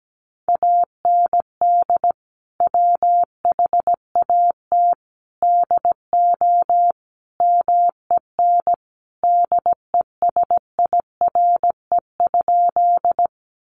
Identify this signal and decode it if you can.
Morse code, heard as a keyed tone, about 17 wpm, 700 Hz. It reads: AND WHAT DO MEN DESIRE?
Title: Sentences from Top 1100 Words 17wpm